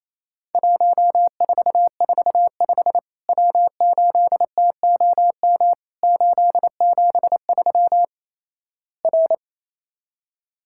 Morse code, heard as a keyed tone, about 28 wpm, 700 Hz. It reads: 1445 W8TOM 873 R
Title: Worked All Europe DX Contest - QTC Exchange - 28wpm